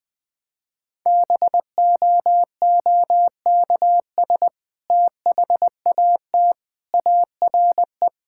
Morse code, heard as a keyed tone, about 21 wpm, 700 Hz. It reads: BOOKS THAT ARE
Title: Sentences with 2-Word Phrases - 20wpm